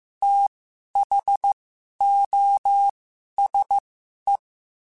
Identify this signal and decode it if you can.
Morse code, heard as a keyed tone, about 15 wpm, spaced as for 11 wpm, 800 Hz. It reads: THOSE
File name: Quotes / Quotes-mp3s / IgnoringFacts @13